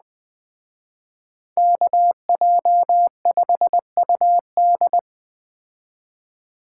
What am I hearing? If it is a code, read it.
Morse code, heard as a keyed tone, about 21 wpm, 700 Hz. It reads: KJ5UD